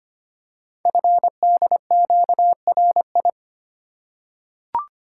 Morse code, heard as a keyed tone, about 25 wpm, 700 Hz. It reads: FDQRI
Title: Code Groups - 5 Letters - 25wpm